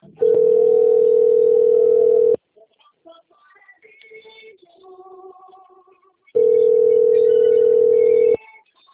Ringback tone (North America), 440 + 480 Hz.
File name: out- - - - - .18034